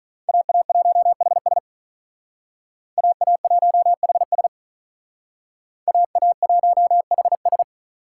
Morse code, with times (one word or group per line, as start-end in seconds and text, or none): 0.3-1.6 AA1HS
3.0-4.5 AA1HS
5.9-7.6 AA1HS